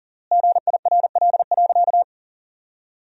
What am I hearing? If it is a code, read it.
Morse code, heard as a keyed tone, about 37 wpm, 700 Hz. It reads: GIRL.
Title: 3x Word-Spacing - Sentences from Top 400 Words - 40wpm